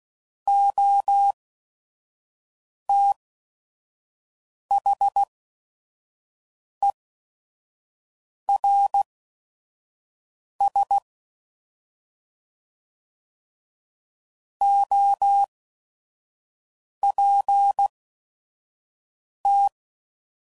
Morse code, heard as a keyed tone, about 16 wpm, spaced as for 5 wpm, 800 Hz. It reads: OTHERS OPT
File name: Prefixes / By First Letter / Os